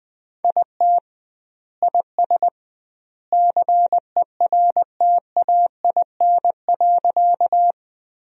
Morse code, heard as a keyed tone, about 21 wpm, 700 Hz. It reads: IT IS CERTAIN.
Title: 2x Word-Spacing - Sentences from Top 500 Words - 20wpm